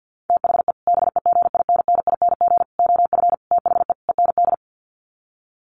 Morse code, heard as a keyed tone, about 50 wpm, 700 Hz. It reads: THE BEGINNING OF THE END
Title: Sets of 5 Words Encore2 50wpm